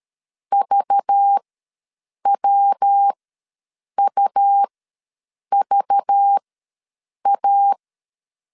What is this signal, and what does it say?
Morse code, heard as a keyed tone, about 13 wpm, spaced as for 7 wpm, 800 Hz. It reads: VWUVA
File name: LEZIONI / 5 LBVC / 5 LBV ws 12wpm 3015